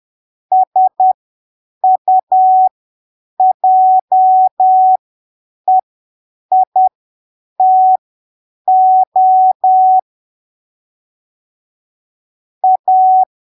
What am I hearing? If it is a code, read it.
Morse code, heard as a keyed tone, about 10 wpm, spaced as for 7 wpm, 750 Hz. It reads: SUJEITO A
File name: arquivos / texto / texto 28 10ppm